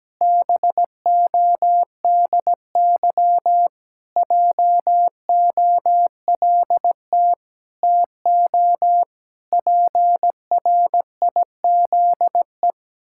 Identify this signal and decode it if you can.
Morse code, heard as a keyed tone, about 17 wpm, 700 Hz. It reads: BODY JOLT TO PRIZE
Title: Pangram Challenge v2 - 17wpm